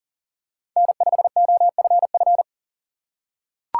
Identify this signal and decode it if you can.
Morse code, heard as a keyed tone, about 40 wpm, 700 Hz. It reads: N5OFF
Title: USA Call Signs - 5 Characters - 2x Character Spacing - 40wpm